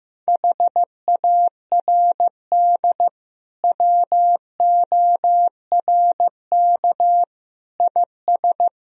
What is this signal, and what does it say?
Morse code, heard as a keyed tone, about 15 wpm, 700 Hz. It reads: HARD WORK IS
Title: Sentences from Top 1000 Words - Review Part 1 - 15wpm